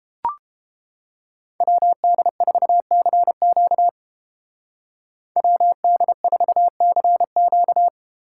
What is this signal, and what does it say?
Morse code, heard as a keyed tone, about 31 wpm, 700 Hz. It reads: WD4CQ WD4CQ